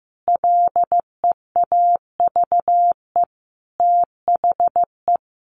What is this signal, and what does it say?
Morse code, heard as a keyed tone, about 15 wpm, 700 Hz. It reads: LEAVE THE
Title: A Sound Of Thunder 15wpm